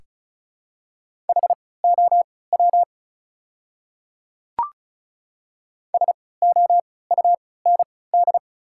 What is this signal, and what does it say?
Morse code, heard as a keyed tone, about 35 wpm, spaced as for 20 wpm, 690 Hz. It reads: HOW SOUND
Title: 3x Character Spacing - Top 100 Words - 35wpm